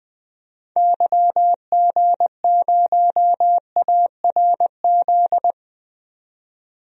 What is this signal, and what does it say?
Morse code, heard as a keyed tone, about 21 wpm, 700 Hz. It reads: YG0ARZ
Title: DXCC-100 Call Signs - 6 Characters - 20wpm